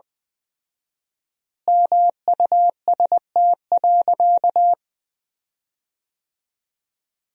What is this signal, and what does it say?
Morse code, heard as a keyed tone, about 20 wpm, 700 Hz. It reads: MUST.
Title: 4x Word-Spacing - Sentences from Top 200 Words - 20wpm